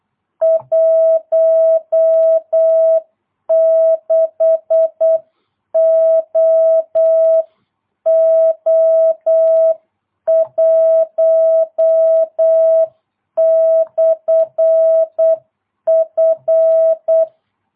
Morse code, heard as a keyed tone, about 8 wpm, 650 Hz. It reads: 16OO1/F